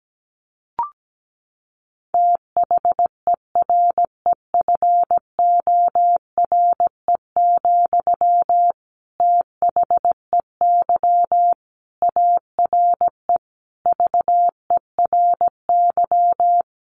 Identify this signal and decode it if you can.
Morse code, heard as a keyed tone, about 17 wpm, 700 Hz. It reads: THEREFORE, THEY ARE VERY
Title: Sentences from Top 1200 Words 17wpm